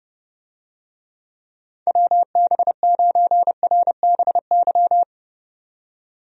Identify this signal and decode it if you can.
Morse code, heard as a keyed tone, about 30 wpm, 700 Hz. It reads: WB9RBY
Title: USA Call Signs - 6 Characters - 30wpm